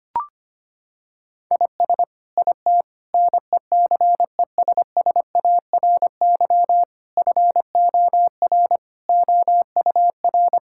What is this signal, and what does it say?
Morse code, heard as a keyed tone, about 25 wpm, 700 Hz. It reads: IS IT NECESSARY FOR OUR
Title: Sentences from Top 1000 Words - Review Part 6 - 25wpm